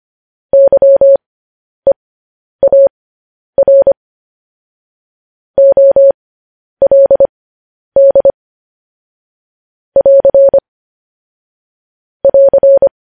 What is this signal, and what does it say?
Morse code, heard as a keyed tone, about 25 wpm, spaced as for 10 wpm, 550 Hz. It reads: YEAR OLD <AR> <AR>
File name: podcast / News headlines-1x-25x10